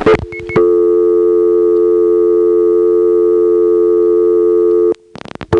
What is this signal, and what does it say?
Dial tone (North America), 350 + 440 Hz.
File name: phonetrips / SmyrnaGA1HQ